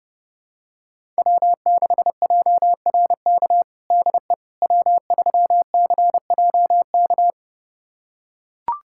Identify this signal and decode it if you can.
Morse code, heard as a keyed tone, about 30 wpm, 700 Hz. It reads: W6JRK DE W3CJK
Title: USA Callsign DE Callsign - 5 Characters - 30wpm